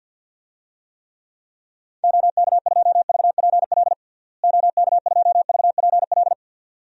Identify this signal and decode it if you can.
Morse code, heard as a keyed tone, about 53 wpm, 700 Hz. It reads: OK2VPL OK2VPL
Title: DXCC-100 Call Signs - 6 Characters - Repeated 2x - 50wpm